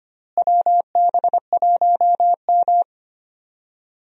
Morse code, heard as a keyed tone, about 25 wpm, 700 Hz. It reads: WB1M
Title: USA Call Signs - 4 Characters - Repeated 2x - 25wpm